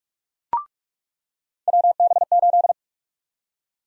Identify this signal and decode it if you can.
Morse code, heard as a keyed tone, about 46 wpm, 700 Hz. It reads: WB8
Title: USA Call Sign Prefixes - 45wpm